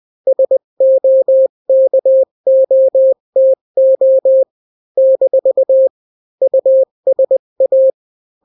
Morse code, heard as a keyed tone, about 20 wpm, 530 Hz. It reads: SOKOTO - USA
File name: daily / 18wpm